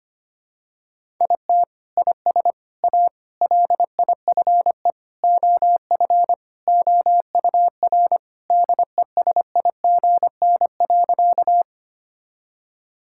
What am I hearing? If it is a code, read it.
Morse code, heard as a keyed tone, about 25 wpm, 700 Hz. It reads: IT IS A LIFE OF OUR DESIGN.
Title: Sentences from Top 1000 Words - Review Part 5 - 25wpm